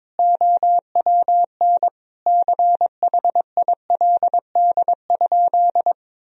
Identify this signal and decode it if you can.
Morse code, heard as a keyed tone, about 22 wpm, 700 Hz. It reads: OWN CHILD?
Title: Sentences from Top 1000 Words - Review Part 2 - 22wpm